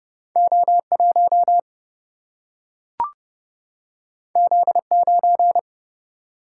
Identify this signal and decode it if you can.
Morse code, heard as a keyed tone, about 30 wpm, 700 Hz. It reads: O 1 Z 9
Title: Single-Letter Single-Number 30wpm